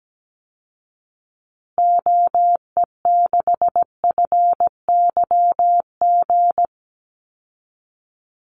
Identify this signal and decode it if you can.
Morse code, heard as a keyed tone, about 17 wpm, 700 Hz. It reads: OE6FYG
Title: Call Signs Repeated 3x - 17wpm